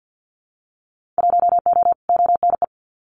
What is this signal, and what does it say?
Morse code, heard as a keyed tone, about 50 wpm, 700 Hz. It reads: 1O ONE
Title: Winter Field Day Exchanges 50wpm